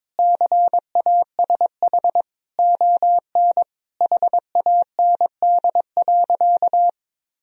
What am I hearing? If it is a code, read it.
Morse code, heard as a keyed tone, about 21 wpm, 700 Hz. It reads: CASH ON HAND.
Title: Sentences from Top 1000 Words - Review Part 4 - 22wpm